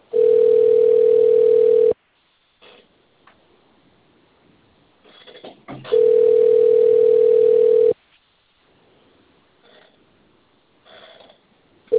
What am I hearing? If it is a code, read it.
Ringback tone (North America), 440 + 480 Hz.